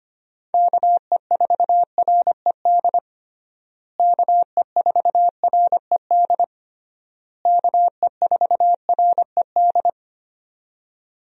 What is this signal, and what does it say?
Morse code, heard as a keyed tone, about 25 wpm, 700 Hz. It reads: KE4RED KE4RED KE4RED
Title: USA Call Signs - 6 Characters - Repeated 3x - 25wpm